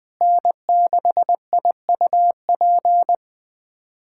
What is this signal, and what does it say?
Morse code, heard as a keyed tone, about 20 wpm, 700 Hz. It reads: N6IUP
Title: USA Callsign DE Callsign 20wpm